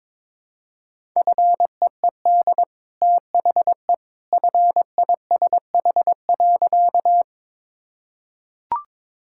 Morse code, heard as a keyed tone, about 22 wpm, 690 Hz. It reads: FEED THE FISH.
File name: practice / 22wpm / Easy Sentences with Hints - 22wpm